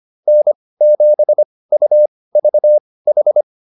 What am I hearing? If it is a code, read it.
Morse code, heard as a keyed tone, about 25 wpm, spaced as for 18 wpm, 600 Hz. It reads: N7UVH